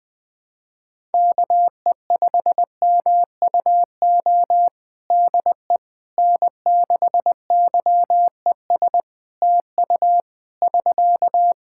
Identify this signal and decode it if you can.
Morse code, heard as a keyed tone, about 20 wpm, 700 Hz. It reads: KE5MUO DE N6YES TU <SK>
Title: K1USN SST QSO Practice 20wpm